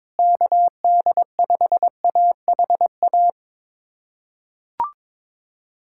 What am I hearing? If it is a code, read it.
Morse code, heard as a keyed tone, about 22 wpm, 700 Hz. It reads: KD5AHA
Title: USA Callsign DE Callsign - 6 Characters - 22wpm